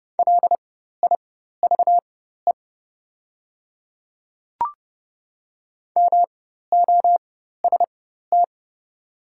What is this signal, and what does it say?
Morse code, heard as a keyed tone, about 30 wpm, spaced as for 14 wpm, 700 Hz. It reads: LIVE MOST
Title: 4x Character Spacing - Top 200 Words - 30wpm